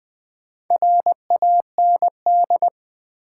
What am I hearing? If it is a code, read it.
Morse code, heard as a keyed tone, about 20 wpm, 700 Hz. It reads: RAND
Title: Build A Word - 2 Letters At A Time - 6-Letter Words - 20wpm